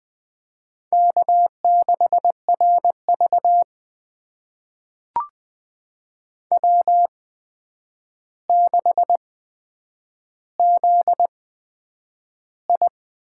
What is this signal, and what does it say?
Morse code, heard as a keyed tone, about 20 wpm, spaced as for 9 wpm, 700 Hz. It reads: K6RV W6ZI
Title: USA Call Signs - 4 Characters - 8x Character Spacing - 20wpm